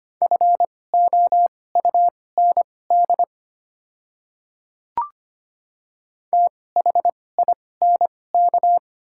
Morse code, heard as a keyed tone, about 25 wpm, spaced as for 18 wpm, 690 Hz. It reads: FOUND THINK
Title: 2x Character Spacing - Top 300 Words - 25wpm